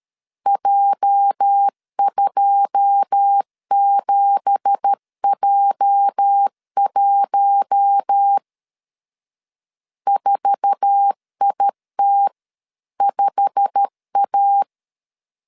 Morse code, heard as a keyed tone, about 13 wpm, 800 Hz. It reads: J27J1 4IT5A
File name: CW / DET / 1 9 1500C S B 12wpm 3819